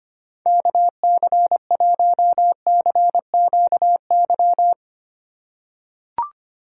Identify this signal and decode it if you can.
Morse code, heard as a keyed tone, about 25 wpm, 700 Hz. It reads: KC1CQY
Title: USA Call Signs - 6 Characters - 3x Character Spacing - 25wpm